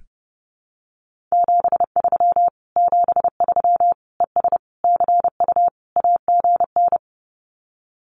Morse code, heard as a keyed tone, about 30 wpm, 700 Hz. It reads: 73 73 ES CU AGN
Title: Common QSO Phrases Encore 30wpm